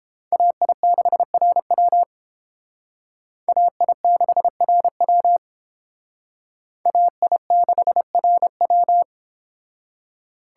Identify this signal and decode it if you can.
Morse code, heard as a keyed tone, about 28 wpm, 700 Hz. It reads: AI6RW AI6RW AI6RW